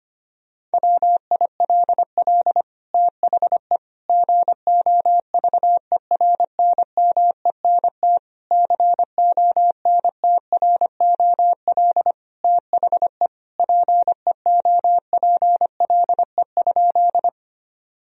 Morse code, heard as a keyed tone, about 25 wpm, 700 Hz. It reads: WILL THE GOVERNMENT CONTROL THE PEOPLE?
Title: Sentences from Top 1000 Words - Review Part 3 - 25wpm